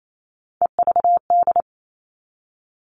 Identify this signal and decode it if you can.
Morse code, heard as a keyed tone, about 28 wpm, 700 Hz. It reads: EVD
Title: All Cut Numbers - 3-Digit 28wpm